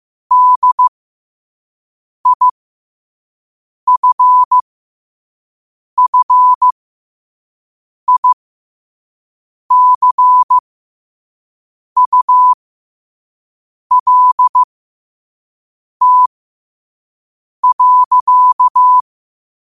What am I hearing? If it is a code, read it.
Morse code, heard as a keyed tone, about 15 wpm, spaced as for 5 wpm, 1000 Hz. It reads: DIFFICULT.
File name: qotdfiles / 5wpm